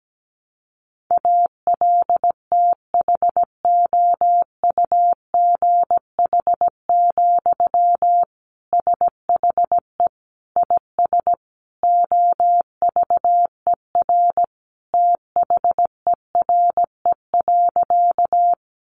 Morse code, heard as a keyed tone, about 17 wpm, 700 Hz. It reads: ALTHOUGH, SHE IS OVER THERE.
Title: Sentences from Top 600 Words 17wpm